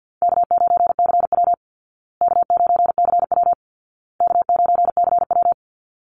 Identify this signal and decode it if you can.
Morse code, heard as a keyed tone, about 48 wpm, 700 Hz. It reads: K9CW K9CW K9CW
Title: Easy Call Signs Repeated 3x - 50wpm